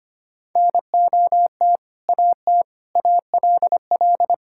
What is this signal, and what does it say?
Morse code, heard as a keyed tone, about 25 wpm, 700 Hz. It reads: NOT AT ALL
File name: practice / 25wpm / Sentences with 2-Word Phrases - 25wpm